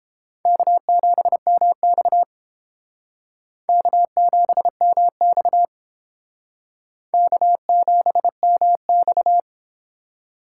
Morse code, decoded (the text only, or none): K7MX K7MX K7MX